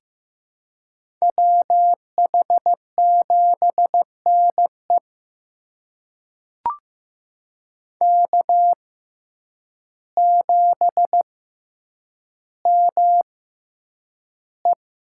Morse code, heard as a keyed tone, about 16 wpm, spaced as for 8 wpm, 700 Hz. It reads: WH7NE K7ME
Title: USA Call Signs - 5 Characters - 6x Character Spacing - 15wpm